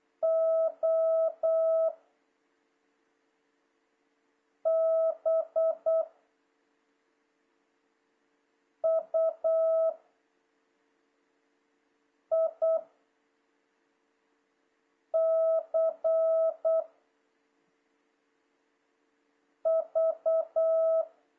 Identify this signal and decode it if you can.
Morse code, heard as a keyed tone, about 8 wpm, 650 Hz. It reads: O B U I C V
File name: jh8cbh / 2026cw